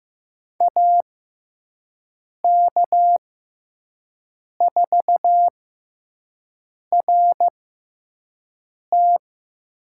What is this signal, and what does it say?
Morse code, heard as a keyed tone, about 15 wpm, spaced as for 5 wpm, 700 Hz. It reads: AK4RT
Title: USA Call Signs - 5 Characters - 6x Character Spacing - 15wpm